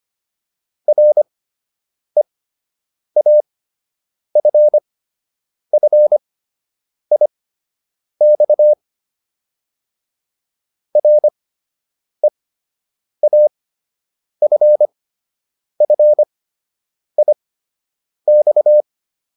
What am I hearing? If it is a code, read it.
Morse code, heard as a keyed tone, about 25 wpm, spaced as for 8 wpm, 600 Hz. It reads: REAFFIX REAFFIX